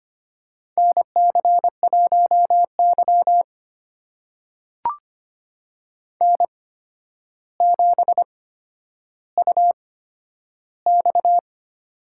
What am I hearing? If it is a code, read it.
Morse code, heard as a keyed tone, about 25 wpm, spaced as for 11 wpm, 700 Hz. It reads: NC1Y N7UX
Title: USA Call Signs - 4 Characters - 8x Character Spacing - 25wpm